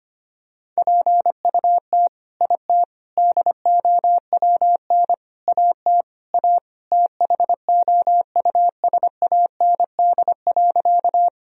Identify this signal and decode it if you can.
Morse code, heard as a keyed tone, about 25 wpm, 700 Hz. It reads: PUT IT DOWN AT A THOUSAND.
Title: Sentences from Top 1000 Words - Review Part 3 - 25wpm